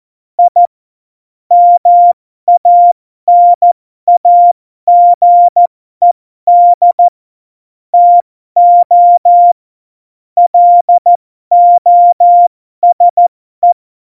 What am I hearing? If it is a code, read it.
Morse code, heard as a keyed tone, about 14 wpm, spaced as for 12 wpm, 700 Hz. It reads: I MANAGED TO LOSE
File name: Morsekurs / Witze